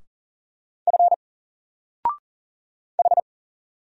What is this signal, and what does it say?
Morse code, heard as a keyed tone, about 40 wpm, 700 Hz. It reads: F H
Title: Single Character (Letter:Number:Punctuation:Prosign) 40wpm